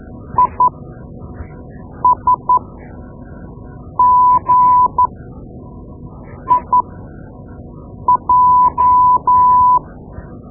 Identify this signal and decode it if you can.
Morse code, heard as a keyed tone, about 12 wpm, spaced as for 5 wpm, 1000 Hz. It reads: ISGIJ
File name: Morse Lessons / Morse Lessons / Tape-2B